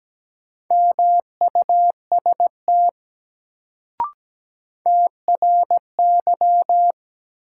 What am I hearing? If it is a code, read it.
Morse code, heard as a keyed tone, about 17 wpm, 700 Hz. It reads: MUST TRY
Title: Sentences from Top 200 Words - New Word Drill - 17wpm